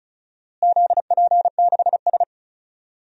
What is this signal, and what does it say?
Morse code, heard as a keyed tone, about 34 wpm, 700 Hz. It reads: ZP6S
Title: DXCC-100 Call Signs - 4 Characters - Repeated 3x - 35wpm